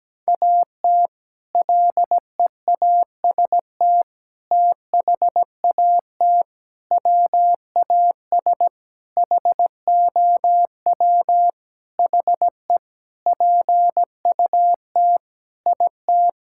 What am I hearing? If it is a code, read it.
Morse code, heard as a keyed tone, about 17 wpm, 700 Hz. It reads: AT LEAST THAT WAS HOW HE PUT IT
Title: Sentences from Top 1600 Words 17wpm